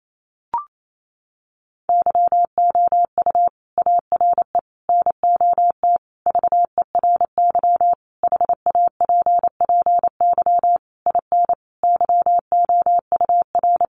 Morse code, heard as a keyed tone, about 28 wpm, 700 Hz. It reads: YOU ARE NOT VERY HAPPY IN YOUR
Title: Sentences from Top 1200 Words 28wpm